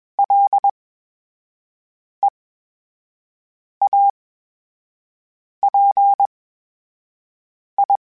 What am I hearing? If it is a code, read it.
Morse code, heard as a keyed tone, about 21 wpm, spaced as for 5 wpm, 800 Hz. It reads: LEAPI
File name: Special INGs-mp3s / Special INGs-mp3s / VerySlowToFaster Ing @20